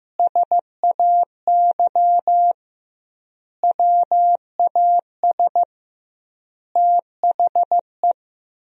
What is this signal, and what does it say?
Morse code, heard as a keyed tone, about 15 wpm, 700 Hz. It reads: SAY WAS THE
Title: 2x Word-Spacing - Sentences from Top 200 Words - 15wpm